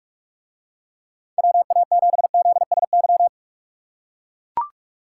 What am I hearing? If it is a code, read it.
Morse code, heard as a keyed tone, about 41 wpm, 700 Hz. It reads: WA7ZSY